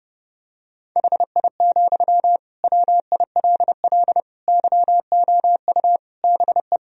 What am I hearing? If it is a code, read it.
Morse code, heard as a keyed tone, about 32 wpm, 700 Hz. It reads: HI, WILL YOU BE
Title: Sentences from Top 1000 Words - Review Part 3 - 30wpm